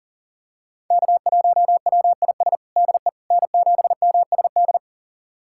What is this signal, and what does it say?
Morse code, heard as a keyed tone, about 40 wpm, 700 Hz. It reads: K1WIS DE N7MSD
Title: USA Callsign DE Callsign 40wpm